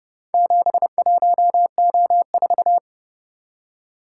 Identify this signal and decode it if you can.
Morse code, heard as a keyed tone, about 29 wpm, 700 Hz. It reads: 71O4